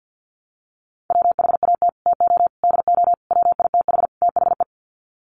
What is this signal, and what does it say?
Morse code, heard as a keyed tone, about 48 wpm, 700 Hz. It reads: WHAT TO DO WITH THE
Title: Sets of 5 Words Encore2 50wpm